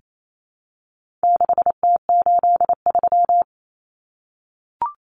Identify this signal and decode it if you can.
Morse code, heard as a keyed tone, about 28 wpm, 700 Hz. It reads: 6T83
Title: Common Cut Numbers - 4-Digit 28wpm